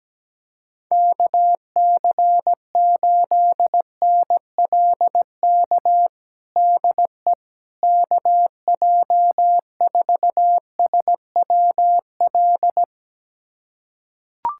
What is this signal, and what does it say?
Morse code, heard as a keyed tone, about 17 wpm, 700 Hz. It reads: KC8NLK DE KJ4SWL
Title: USA Callsign DE Callsign 17wpm